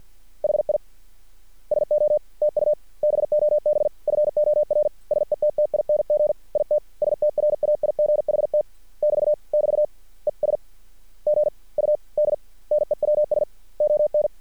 Morse code, heard as a keyed tone, about 50 wpm, spaced as for 44 wpm, 600 Hz. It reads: HI SO TU BOB FOR SETTING IT STRAIGHT <BT> <BT> ES GUD NEWS ON
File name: archived / CRSnet